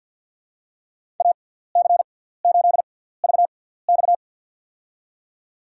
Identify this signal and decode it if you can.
Morse code, heard as a keyed tone, about 47 wpm, spaced as for 17 wpm, 700 Hz. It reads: AC8VX
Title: USA Call Signs - 5 Characters - 6x Character Spacing - 50wpm